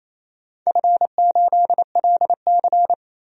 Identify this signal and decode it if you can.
Morse code, heard as a keyed tone, about 28 wpm, 700 Hz. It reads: F8LC
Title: DXCC-100 Call Signs - 4 Characters - 28wpm